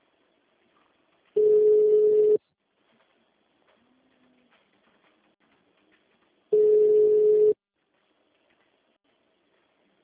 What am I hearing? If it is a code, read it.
Ringback tone (Europe, 425 Hz).
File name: out- -2002- - - .5553